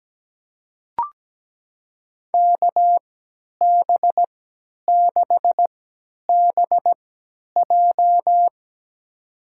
Morse code, heard as a keyed tone, about 17 wpm, spaced as for 10 wpm, 700 Hz. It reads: KB6BJ
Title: USA Call Signs - 5 Characters - 3x Character Spacing - 17wpm